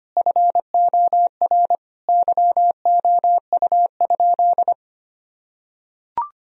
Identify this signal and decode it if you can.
Morse code, heard as a keyed tone, about 25 wpm, 700 Hz. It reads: FOR YOU?
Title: Sentences from Top 1000 Words - Review Part 2 - 25wpm